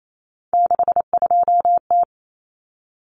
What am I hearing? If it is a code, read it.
Morse code, heard as a keyed tone, about 28 wpm, 700 Hz. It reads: 62T
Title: Common Cut Numbers - 3-Digit 28wpm